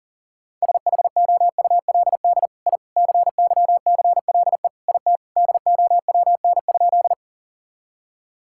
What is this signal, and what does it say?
Morse code, heard as a keyed tone, about 40 wpm, 700 Hz. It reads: SHOULD I CYCLE IT DOWN?
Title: Sentences from Top 1000 Words - Review Part 6 - 40wpm